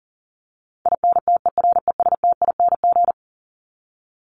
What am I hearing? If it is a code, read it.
Morse code, heard as a keyed tone, about 40 wpm, 700 Hz. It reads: INTERESTING
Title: Top 751-1000 Words 40wpm